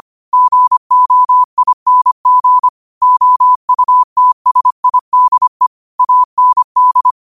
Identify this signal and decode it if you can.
Morse code, heard as a keyed tone, about 25 wpm, 1000 Hz. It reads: GOING OUTSIDE AND
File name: qotdfiles / 25wpm